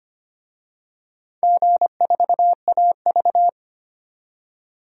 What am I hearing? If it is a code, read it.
Morse code, heard as a keyed tone, about 25 wpm, 700 Hz. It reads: G4AV
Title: DXCC-100 Call Signs - 4 Characters - 25wpm